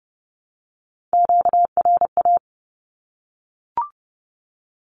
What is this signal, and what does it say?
Morse code, heard as a keyed tone, about 30 wpm, 700 Hz. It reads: QRA
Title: Q-Codes 30wpm